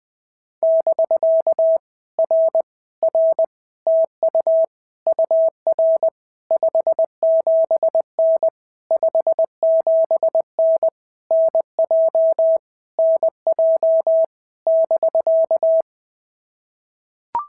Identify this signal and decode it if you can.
Morse code, heard as a keyed tone, about 19 wpm, 650 Hz. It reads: <BK> R R TU UR 57N 57N NJ NJ <BK>
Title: POTA QSO Practice - 20wpm